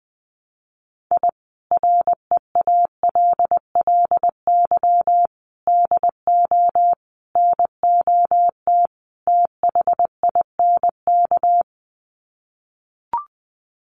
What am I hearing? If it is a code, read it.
Morse code, heard as a keyed tone, about 20 wpm, 700 Hz. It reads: I REALLY DO NOT THINK
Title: Sets of 5 Words Encore2 20wpm